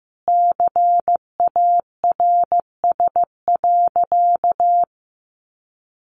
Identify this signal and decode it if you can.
Morse code, heard as a keyed tone, about 15 wpm, 700 Hz. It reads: CARS.
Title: Easy Phrases 15wpm